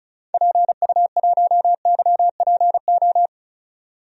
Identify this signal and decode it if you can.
Morse code, heard as a keyed tone, about 34 wpm, 700 Hz. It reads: PU1YPO
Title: DXCC-100 Call Signs - 6 Characters - 35wpm